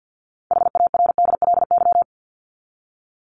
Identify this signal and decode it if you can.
Morse code, heard as a keyed tone, about 50 wpm, 700 Hz. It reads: HARDLY
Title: Top 1751-2000 Words 50wpm